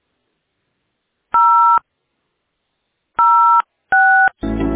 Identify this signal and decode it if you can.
DTMF, heard as telephone touch tones, tones of about 440 ms. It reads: 0 06